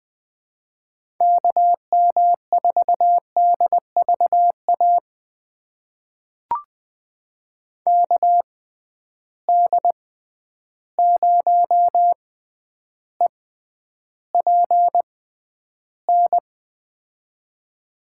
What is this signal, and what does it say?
Morse code, heard as a keyed tone, about 21 wpm, spaced as for 10 wpm, 700 Hz. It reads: KM4DVA KD0EPN
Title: USA Call Signs - 6 Characters - 6x Character Spacing - 20wpm